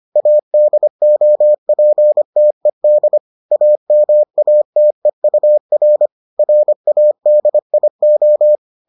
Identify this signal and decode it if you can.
Morse code, heard as a keyed tone, about 25 wpm, 600 Hz. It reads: ADOPTED AMATEUR RADIO